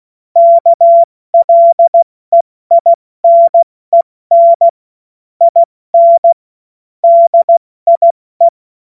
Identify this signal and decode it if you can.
Morse code, heard as a keyed tone, about 16 wpm, 680 Hz. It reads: KLEINEN IN DIE
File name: Witz-16-September-2025-70-F80